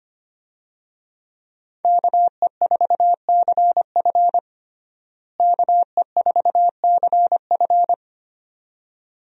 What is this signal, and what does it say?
Morse code, heard as a keyed tone, about 25 wpm, 700 Hz. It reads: KE4CF KE4CF
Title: USA Call Signs - 5 Characters - Repeated 2x - 25wpm